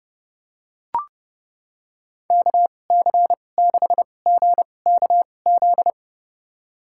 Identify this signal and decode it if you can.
Morse code, heard as a keyed tone, about 30 wpm, spaced as for 22 wpm, 700 Hz. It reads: KC6GKZ
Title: USA Call Signs - 6 Characters - 2x Character Spacing - 30wpm